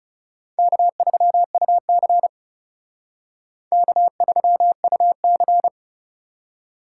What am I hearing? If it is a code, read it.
Morse code, heard as a keyed tone, about 37 wpm, 700 Hz. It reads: K3UC K3UC
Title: Speed Racing - USA 1x2 Call Signs 30wpm